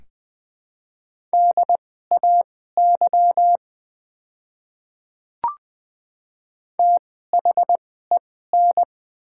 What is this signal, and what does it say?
Morse code, heard as a keyed tone, about 20 wpm, spaced as for 14 wpm, 690 Hz. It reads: DAY THEN
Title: 2x Character Spacing - Top 100 Words - 20wpm